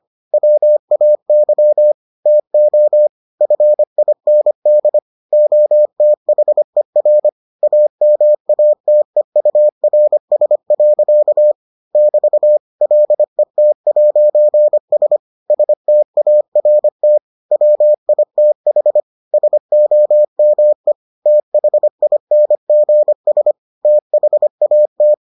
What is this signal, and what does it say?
Morse code, heard as a keyed tone, about 25 wpm, 600 Hz. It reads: WAY TO FIND OTHER AMATEURS. <BT> LET'S START WITH SOME THINGS THAT